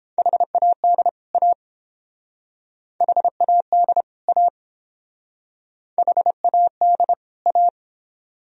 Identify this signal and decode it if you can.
Morse code, heard as a keyed tone, about 30 wpm, 700 Hz. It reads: HAD A HAD A HAD A